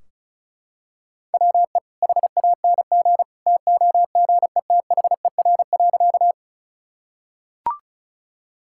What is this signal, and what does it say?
Morse code, heard as a keyed tone, about 36 wpm, 700 Hz. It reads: WE HANG TOGETHER.
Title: Sentences from Top 1600 Words 35wpm